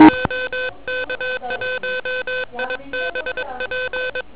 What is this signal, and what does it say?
Morse code, heard as a keyed tone, about 22 wpm, 3550 Hz, with noise. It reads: OK1IBP